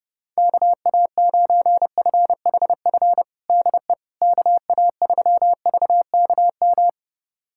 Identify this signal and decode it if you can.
Morse code, heard as a keyed tone, about 28 wpm, 700 Hz. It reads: KA9FHF DE KA3VKM
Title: USA Callsign DE Callsign - 6 Characters - 30wpm